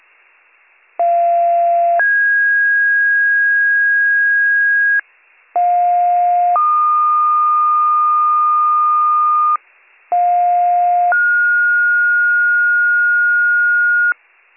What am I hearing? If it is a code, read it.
Two-tone page, heard as a one-second tone, then a three-second tone, about 690 then 1650 Hz, about 710 then 1220 Hz, about 710 then 1430 Hz, with noise. It is heard from 1.0 to 14.1 s.